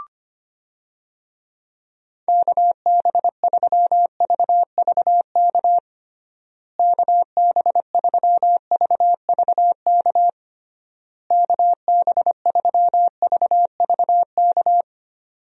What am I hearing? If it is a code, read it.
Morse code, heard as a keyed tone, about 25 wpm, 700 Hz. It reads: KB3VVK KB3VVK KB3VVK
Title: USA Call Signs - 6 Characters - Repeated 3x - 25wpm